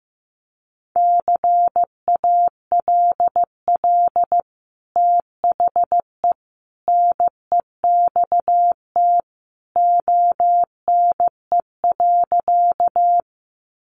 Morse code, heard as a keyed tone, about 15 wpm, 700 Hz. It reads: CALL THE NEXT ONE.
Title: Sentences from Top 300 Words 15wpm